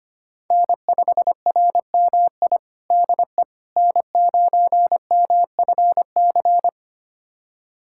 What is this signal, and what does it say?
Morse code, heard as a keyed tone, about 25 wpm, 700 Hz. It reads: N5RMI DE N9MFC
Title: USA Callsign DE Callsign - 5 Characters - 25wpm